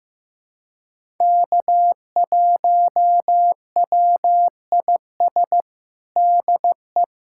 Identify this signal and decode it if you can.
Morse code, heard as a keyed tone, about 15 wpm, 700 Hz. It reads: K1WIS DE
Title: USA Callsign DE Callsign 15wpm